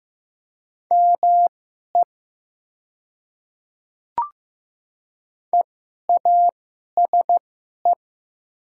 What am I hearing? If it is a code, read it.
Morse code, heard as a keyed tone, about 15 wpm, spaced as for 11 wpm, 690 Hz. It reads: ME EASE
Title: 2x Character Spacing - Top 300 Words - 15wpm